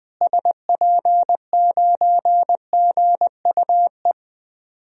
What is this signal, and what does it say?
Morse code, heard as a keyed tone, about 20 wpm, 700 Hz. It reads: SP9GUE
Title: DXCC-100 Call Signs - 6 Characters - Repeated 3x - 20wpm